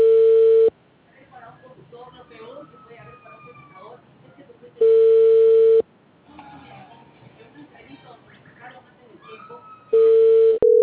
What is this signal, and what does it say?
Ringback tone (China, 450 Hz), with speech.